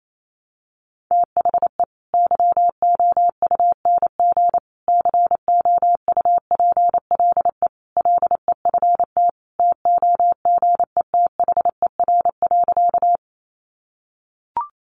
Morse code, heard as a keyed tone, about 28 wpm, 700 Hz. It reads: THE YOUNG COUPLE LEFT TOGETHER.
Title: Sentences from Top 1100 Words 28wpm